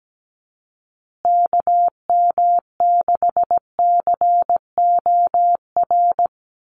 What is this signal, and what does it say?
Morse code, heard as a keyed tone, about 17 wpm, 700 Hz. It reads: KM6COR
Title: USA Call Signs Repeated 3x 17wpm